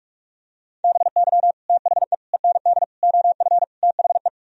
Morse code, heard as a keyed tone, about 45 wpm, 700 Hz. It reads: BY THE END OF THE